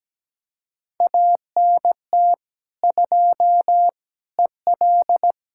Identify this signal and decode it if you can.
Morse code, heard as a keyed tone, about 17 wpm, 700 Hz. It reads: ANT 2 EL
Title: Speed-Racing Common QSO Phrases - 17wpm